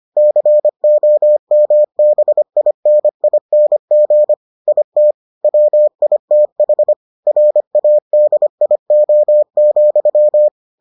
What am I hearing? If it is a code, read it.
Morse code, heard as a keyed tone, about 25 wpm, 600 Hz. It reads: COMBINING IT WITH RADIO,